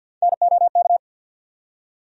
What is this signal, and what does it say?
Morse code, heard as a keyed tone, about 47 wpm, 700 Hz. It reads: NOK
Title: USA Call Sign Postfixes - 3 Letters 50wpm